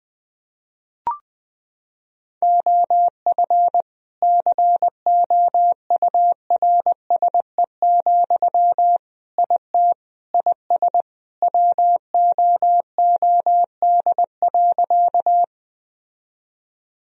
Morse code, heard as a keyed tone, about 20 wpm, 700 Hz. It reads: OF COURSE, IT IS WOOD.
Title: Sentences from Top 1000 Words - Review Part 2 - 20wpm